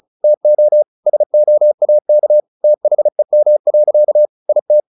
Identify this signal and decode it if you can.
Morse code, heard as a keyed tone, about 34 wpm, 600 Hz. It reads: TO SOAK THEM. IT
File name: the old man and the sea 2 35wpm